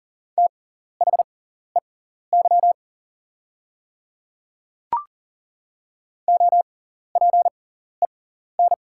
Morse code, heard as a keyed tone, about 40 wpm, spaced as for 14 wpm, 690 Hz. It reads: THEY OPEN
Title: 6x Character Spacing - Top 300 Words - 40wpm